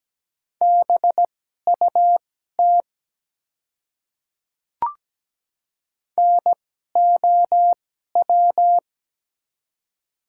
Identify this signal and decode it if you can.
Morse code, heard as a keyed tone, about 17 wpm, spaced as for 12 wpm, 700 Hz. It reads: BUT NOW
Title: 2x Character Spacing - Top 200 Words - 17wpm